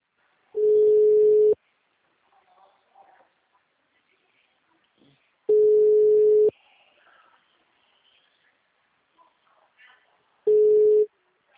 Ringback tone (Europe, 425 Hz).